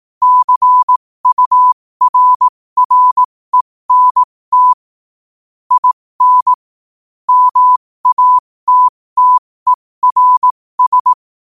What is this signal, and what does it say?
Morse code, heard as a keyed tone, about 18 wpm, spaced as for 15 wpm, 1000 Hz. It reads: CURRENT IN MATTERS
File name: qotdfiles / 15wpm